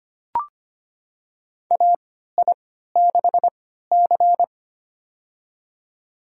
Morse code, heard as a keyed tone, about 25 wpm, spaced as for 14 wpm, 700 Hz. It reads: AI6C
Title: USA Call Signs - 4 Characters - 3x Character Spacing - 25wpm